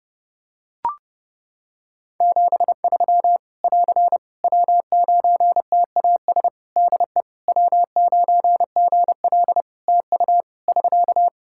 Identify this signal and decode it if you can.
Morse code, heard as a keyed tone, about 30 wpm, 700 Hz. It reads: 73 <AR> W9TAS DE W9GL TU <SK>
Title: Practice QSO - Exchange 4 - Ending - 30wpm